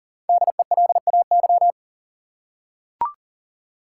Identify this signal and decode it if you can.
Morse code, heard as a keyed tone, about 40 wpm, 700 Hz. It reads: DELAY
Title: Five Letter Words - 40wpm